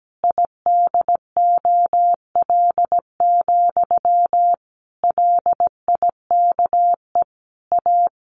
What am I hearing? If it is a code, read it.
Morse code, heard as a keyed tone, about 17 wpm, 700 Hz. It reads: IDOL, LIKE A
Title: A Sound Of Thunder 17wpm